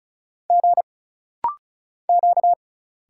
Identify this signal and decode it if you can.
Morse code, heard as a keyed tone, about 36 wpm, 700 Hz. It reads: G Q
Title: Single Letter-Number Alt - Rapid-Fire 35wpm